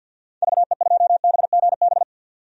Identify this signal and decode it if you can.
Morse code, heard as a keyed tone, about 51 wpm, 700 Hz. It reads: VE2BGB
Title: DXCC-100 Call Signs - 6 Characters - 50wpm